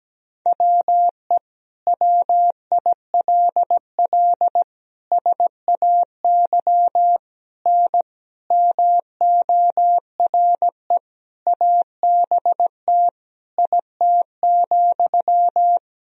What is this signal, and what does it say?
Morse code, heard as a keyed tone, about 17 wpm, 700 Hz. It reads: WE WILL SAY N MORE ABT IT,
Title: Sentences from Top 200 Words and Abbreviations 17wpm